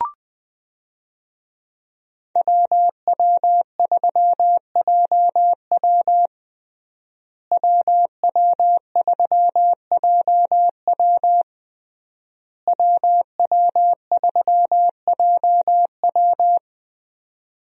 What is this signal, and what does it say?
Morse code, heard as a keyed tone, about 20 wpm, 700 Hz. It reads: WW3JW WW3JW WW3JW
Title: USA Call Signs - 5 Characters - Repeated 3x - 20wpm